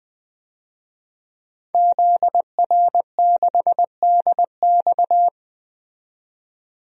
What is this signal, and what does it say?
Morse code, heard as a keyed tone, about 20 wpm, 700 Hz. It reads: ZR6DX
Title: DXCC-100 Call Signs - 5 Characters - 20wpm